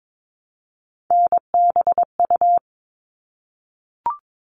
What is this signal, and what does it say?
Morse code, heard as a keyed tone, about 22 wpm, 700 Hz. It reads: NBU
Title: All Cut Numbers - 3-Digit 22wpm